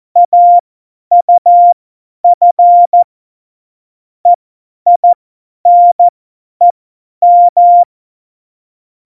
Morse code, heard as a keyed tone, about 14 wpm, spaced as for 10 wpm, 700 Hz. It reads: AUF EINEM
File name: Morsekurs / Witze / Witz 12 Mai 2020, 50 F70